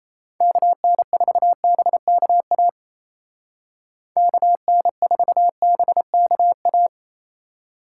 Morse code, heard as a keyed tone, about 33 wpm, 700 Hz. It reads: KN4BKA KN4BKA
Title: Speed Racing - USA 2x3 Call Signs 28wpm